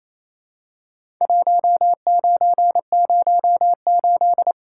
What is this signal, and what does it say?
Morse code, heard as a keyed tone, about 28 wpm, 700 Hz. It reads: 1908
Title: Worked All Europe DX Contest - QTC Exchange - 28wpm